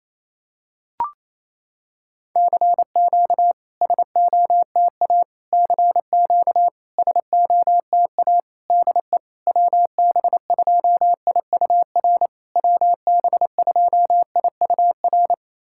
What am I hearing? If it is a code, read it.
Morse code, heard as a keyed tone, about 28 wpm, 700 Hz. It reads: CQ SOTA CQ SOTA DE WB2IUR WB2IUR
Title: SOTA QSO Practice 28wpm